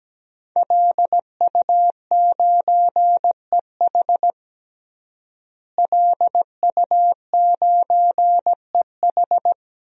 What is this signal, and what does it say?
Morse code, heard as a keyed tone, about 17 wpm, 700 Hz. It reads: LU9EH LU9EH
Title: DXCC-100 Call Signs - 5 Characters - Repeated 3x - 17wpm